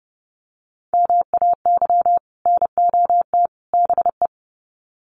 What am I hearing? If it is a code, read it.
Morse code, heard as a keyed tone, about 30 wpm, 700 Hz. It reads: MAY NOT BE
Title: Sets of 3 Words 30wpm